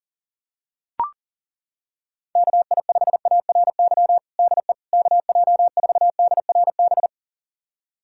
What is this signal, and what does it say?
Morse code, heard as a keyed tone, about 40 wpm, 700 Hz. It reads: KI5ARY DE KJ4DRB
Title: USA Callsign DE Callsign 40wpm